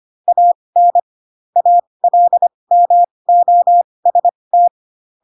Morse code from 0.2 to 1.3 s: AN